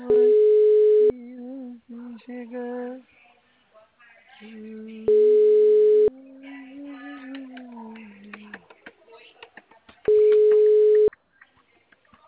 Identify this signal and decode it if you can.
Ringback tone (Europe, 425 Hz).
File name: out- -2002- - - .21995